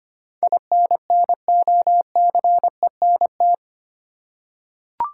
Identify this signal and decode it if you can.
Morse code, heard as a keyed tone, about 25 wpm, 700 Hz. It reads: INNOCENT
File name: 8 Letter Words - First 3 Letters Given - 25wpm